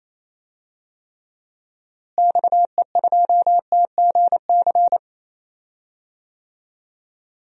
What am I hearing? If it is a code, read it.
Morse code, heard as a keyed tone, about 28 wpm, 700 Hz. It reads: XE2TGC
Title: DXCC-100 Call Signs - 6 Characters - Repeated 3x - 28wpm